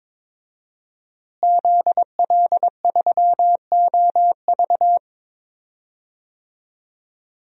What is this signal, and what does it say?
Morse code, heard as a keyed tone, about 22 wpm, 700 Hz. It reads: ZL3OV
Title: DXCC-100 Call Signs - 5 Characters - Repeated 3x - 22wpm